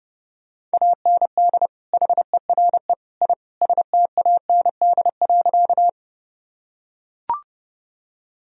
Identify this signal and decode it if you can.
Morse code, heard as a keyed tone, about 30 wpm, 700 Hz. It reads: AND HERE I STAND.
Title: Sentences from Top 1000 Words - Review Part 1 - 30wpm